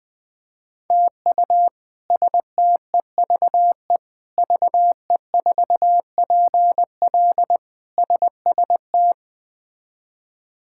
Morse code, heard as a keyed tone, about 20 wpm, 700 Hz. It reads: TU STEVE VE4PL SST
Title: K1USN SST QSO Practice - 20wpm